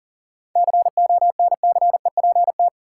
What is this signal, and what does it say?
Morse code, heard as a keyed tone, about 40 wpm, 700 Hz. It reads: CONCEPT
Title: Seven Letter Words - 40wpm